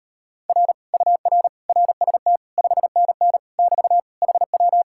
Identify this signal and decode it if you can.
Morse code, heard as a keyed tone, about 38 wpm, 700 Hz. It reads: R UR RST 5NN <BT> HW